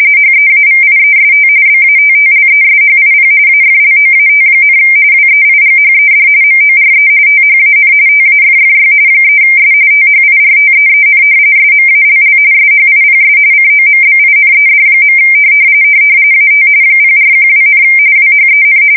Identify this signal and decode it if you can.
RTTY, about 45.45 baud. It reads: HEAVIEST BATTLESHIP AFLOAT, WEIGHED MORE THAN TAFFY 3S THIRTEEN SMALL SHIPS COMBINED. AT 0412, WHEN
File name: fsk170 / battleshipnj